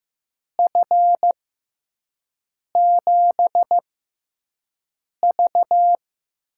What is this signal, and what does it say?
Morse code, heard as a keyed tone, about 15 wpm, spaced as for 5 wpm, 700 Hz. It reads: F7V
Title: USA Call Signs - 6x Character Spacing - 15wpm